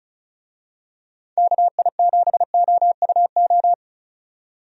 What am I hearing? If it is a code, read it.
Morse code, heard as a keyed tone, about 34 wpm, 700 Hz. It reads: KI7OUO
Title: USA Call Signs - 6 Characters - 35wpm